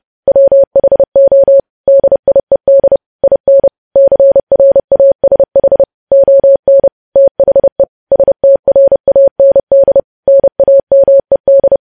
Morse code, heard as a keyed tone, about 30 wpm, 550 Hz. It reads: WHO DIED IN CRASH ON THE STRAND NAMED
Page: news headlines in morse code at 30 wpm